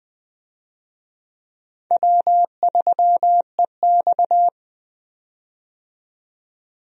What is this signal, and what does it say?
Morse code, heard as a keyed tone, about 20 wpm, 700 Hz. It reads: W3EX
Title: USA Call Signs - 4 Characters - 20wpm